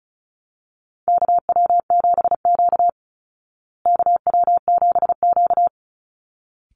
Morse code, heard as a keyed tone, about 34 wpm, 700 Hz. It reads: KW7Q KW7Q
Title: Call Signs Repeated 3x - 35wpm